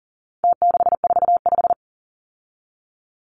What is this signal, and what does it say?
Morse code, heard as a keyed tone, about 40 wpm, 700 Hz. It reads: T645
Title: Numbers - Contest Style - 4-Digit 40wpm